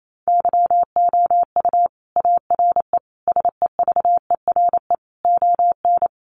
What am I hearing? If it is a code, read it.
Morse code, heard as a keyed tone, about 28 wpm, 700 Hz. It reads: YOU ARE SEVERE ON
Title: Sentences from Top 1400 Words 28wpm